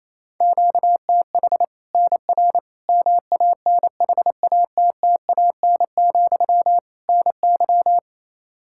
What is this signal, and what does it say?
Morse code, heard as a keyed tone, about 28 wpm, 700 Hz. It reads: QTH NR MANHATTAN, NY
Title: Practice QSO - Exchange 1 - Beginning - 28wpm